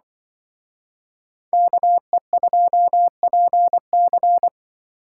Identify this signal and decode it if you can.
Morse code, heard as a keyed tone, about 24 wpm, 700 Hz. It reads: KE2PC
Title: Speed Racing - USA 2x2 Call Signs 20wpm